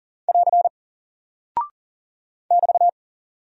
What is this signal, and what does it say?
Morse code, heard as a keyed tone, about 40 wpm, 700 Hz. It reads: <AR> <BT>
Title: Single Character (Letter:Number:Punctuation:Prosign) Alt 40wpm